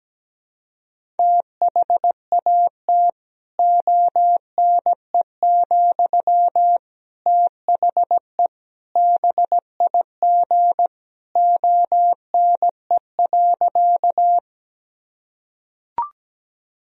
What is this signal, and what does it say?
Morse code, heard as a keyed tone, about 17 wpm, 700 Hz. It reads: THAT ONE, THE BIG ONE.
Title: Sentences from Top 1000 Words - Review Part 1 - 17wpm